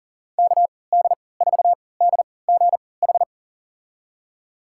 Morse code, heard as a keyed tone, about 40 wpm, spaced as for 23 wpm, 700 Hz. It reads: KD4DGH
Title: USA Call Signs - 3x Character Spacing - 40wpm